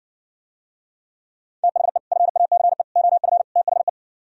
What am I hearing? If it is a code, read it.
Morse code, heard as a keyed tone, about 60 wpm, 700 Hz. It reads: THE FACE OF THE